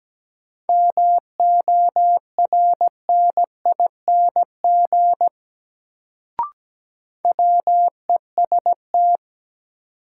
Morse code, heard as a keyed tone, about 17 wpm, 700 Hz. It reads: MORNING WEST
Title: Sentences from Top 400 Words - New Word Drill - 17wpm